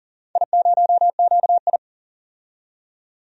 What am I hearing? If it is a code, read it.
Morse code, heard as a keyed tone, about 38 wpm, 700 Hz. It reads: I0QI
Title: DXCC-100 Call Signs - 4 Characters - 40wpm